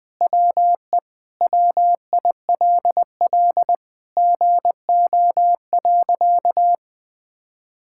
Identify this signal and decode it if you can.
Morse code, heard as a keyed tone, about 20 wpm, 700 Hz. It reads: WE WILL GO.
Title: Sentences with 4-Word Phrases - 20wpm